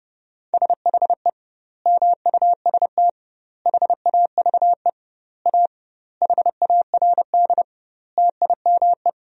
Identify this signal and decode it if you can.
Morse code, heard as a keyed tone, about 30 wpm, 700 Hz. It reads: SHE MUST HAVE A HARD TIME
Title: 2x Word-Spacing - Sentences from Top 300 Words - 30wpm